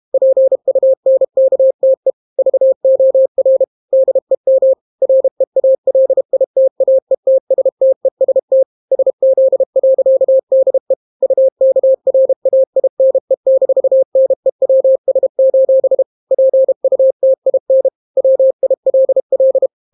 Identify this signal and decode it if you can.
Morse code, heard as a keyed tone, about 32 wpm, 530 Hz. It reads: PUNKTE VOR DEM REALITAETSTEST SZ.DE UKRAINE-NEWS: PUTIN WILL